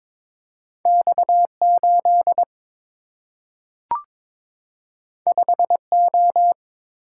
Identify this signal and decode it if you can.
Morse code, heard as a keyed tone, about 22 wpm, 700 Hz. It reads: X8 5O
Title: Single-Letter Single-Number 22wpm